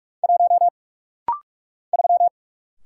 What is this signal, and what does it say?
Morse code, heard as a keyed tone, about 44 wpm, 700 Hz. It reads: 1 3
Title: Single Letter-Number - Rapid-Fire 45wpm